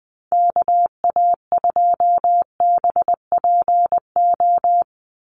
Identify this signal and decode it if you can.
Morse code, heard as a keyed tone, about 20 wpm, 700 Hz. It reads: KA2BPO
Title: USA Call Signs 20wpm